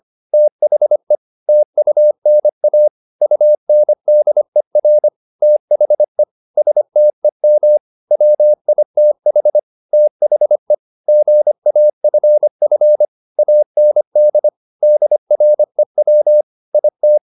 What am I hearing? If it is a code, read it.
Morse code, heard as a keyed tone, about 25 wpm, 600 Hz. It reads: THE TUNA UNDER THE STEM WITH THE GAFF AND DREW IT
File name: the old man and the sea part 1 25wpm